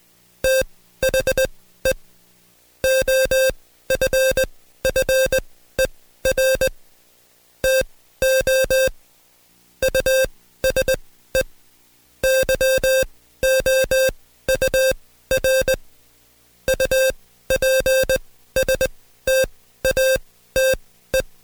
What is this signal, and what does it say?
Morse code, heard as a keyed tone, about 20 wpm, spaced as for 13 wpm, 530 Hz. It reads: THE OFFER TO USE YOUR UPSTATE
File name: Code Quick / AudioDownld2 / Seg133-Stepper6 13WPM